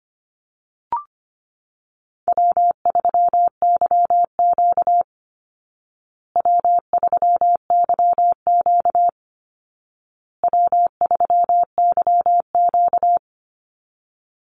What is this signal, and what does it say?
Morse code, heard as a keyed tone, about 25 wpm, 700 Hz. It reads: W3YQ W3YQ W3YQ
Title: Easy Call Signs Repeated 3x - 25wpm